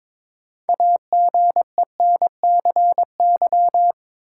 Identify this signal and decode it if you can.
Morse code, heard as a keyed tone, about 22 wpm, 700 Hz. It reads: AGENCY
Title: Sentences from Top 1300 Words - New Word Drill Alt - 22wpm